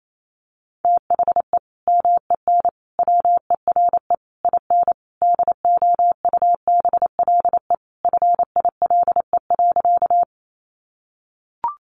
Morse code, heard as a keyed tone, about 28 wpm, 700 Hz. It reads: THE MEN WERE IN DOUBLE FILE.
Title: Sentences from Top 1400 Words 28wpm